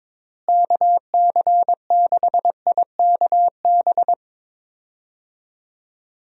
Morse code, heard as a keyed tone, about 22 wpm, 700 Hz. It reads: KC6IKB
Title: USA Call Signs - 6 Characters - Repeated 3x - 22wpm